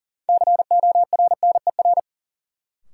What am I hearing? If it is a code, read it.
Morse code, heard as a keyed tone, about 40 wpm, 700 Hz. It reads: CORNER
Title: Sentences from Top 1600 Words - New Word Drill Alt - 40wpm